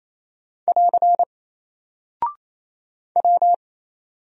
Morse code, heard as a keyed tone, about 27 wpm, 700 Hz. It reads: <AR> W
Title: Single Character (Letter:Number:Punctuation:Prosign) Alt 28wpm